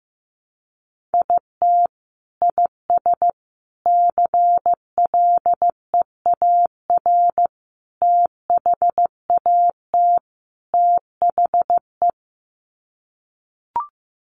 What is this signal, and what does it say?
Morse code, heard as a keyed tone, about 15 wpm, 700 Hz. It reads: IT IS CLEAR THAT THE
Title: Sets of 5 Words Encore 15wpm